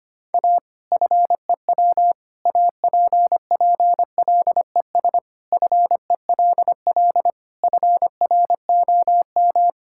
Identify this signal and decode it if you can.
Morse code, heard as a keyed tone, about 25 wpm, 700 Hz. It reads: A FEW APPLES FELL FROM
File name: practice / 25wpm / Sentences with 2-Word Phrases - 25wpm